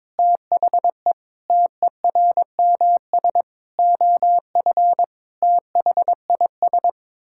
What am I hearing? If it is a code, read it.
Morse code, heard as a keyed tone, about 22 wpm, 700 Hz. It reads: THE TERMS OF THIS
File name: Sets of 9 Words - 22wpm